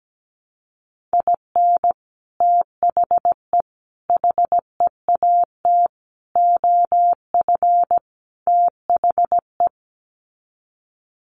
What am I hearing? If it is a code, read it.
Morse code, heard as a keyed tone, about 17 wpm, 700 Hz. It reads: IN THE HEAT OF THE
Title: Sets of 5 Words Encore2 17wpm